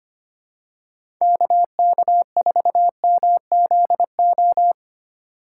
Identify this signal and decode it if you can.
Morse code, heard as a keyed tone, about 25 wpm, 700 Hz. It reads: KK4MZO
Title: USA Call Signs - 2x Character Spacing - 25wpm